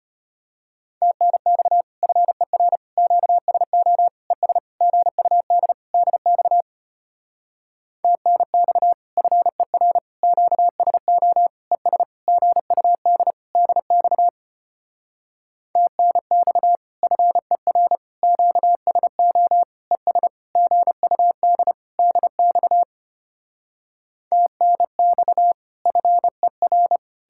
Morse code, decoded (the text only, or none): TNX FER QSO ES GUD DX TNX FER QSO ES GUD DX TNX FER QSO ES GUD DX TNX FER